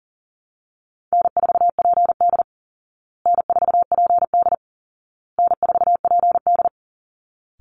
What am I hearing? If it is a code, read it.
Morse code, heard as a keyed tone, about 40 wpm, 700 Hz. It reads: N4PD N4PD N4PD
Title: Call Signs Repeated 2x - 40wpm